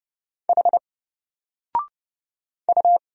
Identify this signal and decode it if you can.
Morse code, heard as a keyed tone, about 29 wpm, 700 Hz. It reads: H U
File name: Single-Letter-Number-30wpm